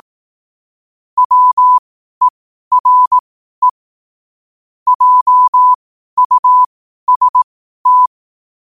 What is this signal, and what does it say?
Morse code, heard as a keyed tone, about 18 wpm, spaced as for 12 wpm, 1000 Hz. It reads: WERE JUST
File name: qotdfiles / 12wpm